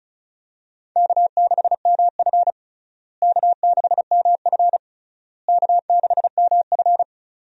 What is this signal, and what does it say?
Morse code, heard as a keyed tone, about 34 wpm, 700 Hz. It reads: K6MF K6MF K6MF
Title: USA Call Signs - 4 Characters - Repeated 3x - 35wpm